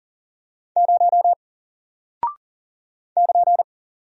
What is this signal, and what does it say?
Morse code, heard as a keyed tone, about 40 wpm, 700 Hz. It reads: 0 <KN>
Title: Single Character (Letter:Number:Punctuation:Prosign) Alt 40wpm